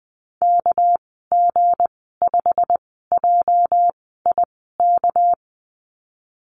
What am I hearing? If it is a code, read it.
Morse code, heard as a keyed tone, about 20 wpm, spaced as for 14 wpm, 700 Hz. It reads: KG5JIK
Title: 245 Lesson 39g - Letter z - Call Signs 20wpm